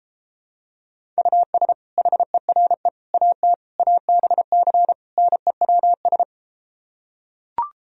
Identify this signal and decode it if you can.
Morse code, heard as a keyed tone, about 33 wpm, 700 Hz. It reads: US HERE AT ABC NEWS